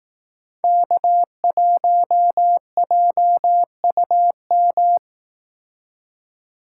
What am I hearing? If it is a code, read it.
Morse code, heard as a keyed tone, about 18 wpm, 700 Hz. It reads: K1JUM
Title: Speed Racing - USA 1x3 Call Signs 15wpm